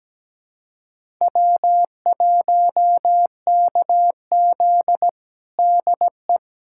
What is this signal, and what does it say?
Morse code, heard as a keyed tone, about 17 wpm, 700 Hz. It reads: W1KZ DE
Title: USA Callsign DE Callsign - 4 Characters - 17wpm